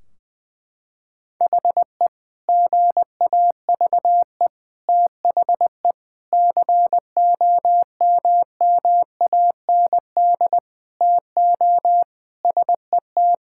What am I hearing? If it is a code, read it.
Morse code, heard as a keyed tone, about 20 wpm, 700 Hz. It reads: HE GAVE THE COMMAND TO SET
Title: Sentences from Top 1600 Words 20wpm